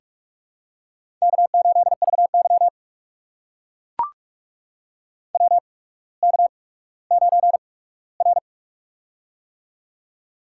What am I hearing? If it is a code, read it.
Morse code, heard as a keyed tone, about 44 wpm, spaced as for 19 wpm, 700 Hz. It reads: K8VY WK9R
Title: USA Call Signs - 4 Characters - 8x Character Spacing - 45wpm